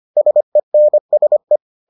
Morse code, heard as a keyed tone, about 25 wpm, 600 Hz. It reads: SENSE